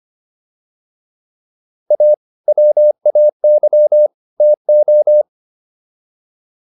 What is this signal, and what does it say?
Morse code, heard as a keyed tone, about 25 wpm, 600 Hz. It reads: A WAY TO